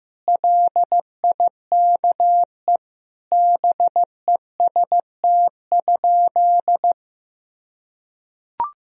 Morse code, heard as a keyed tone, about 15 wpm, 700 Hz. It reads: LIKE BEST?
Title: Sentences from Top 1000 Words - Review Part 1 - 15wpm